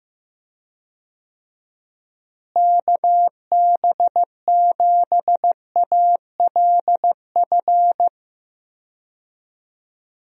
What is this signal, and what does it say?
Morse code, heard as a keyed tone, about 15 wpm, 700 Hz. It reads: KB7ALF
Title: USA Call Signs - 6 Characters - Repeated 3x - 15wpm